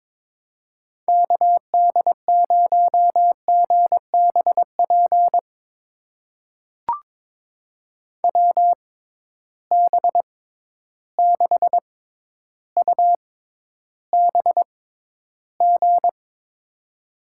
Morse code, heard as a keyed tone, about 22 wpm, spaced as for 11 wpm, 700 Hz. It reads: KD0GBP WB6UBG
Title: USA Call Signs - 6x Character Spacing - 22wpm